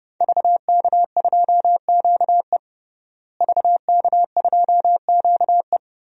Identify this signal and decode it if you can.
Morse code, heard as a keyed tone, about 30 wpm, 700 Hz. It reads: VK2QE VK2QE
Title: DXCC-100 Call Signs - 5 Characters - Repeated 3x - 30wpm